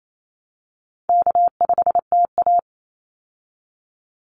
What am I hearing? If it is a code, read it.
Morse code, heard as a keyed tone, about 28 wpm, 700 Hz. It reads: K5TA
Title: Call Signs Repeated 2x - 28wpm